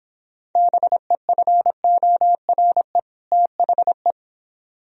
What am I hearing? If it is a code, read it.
Morse code, heard as a keyed tone, about 29 wpm, 700 Hz. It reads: BEFORE THE